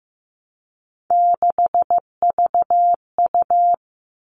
Morse code, heard as a keyed tone, about 15 wpm, 700 Hz. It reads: 6VU
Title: All Cut Numbers - 3-Digit 15wpm